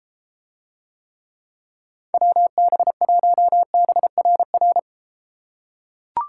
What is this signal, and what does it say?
Morse code, heard as a keyed tone, about 33 wpm, 700 Hz. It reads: WB1BRR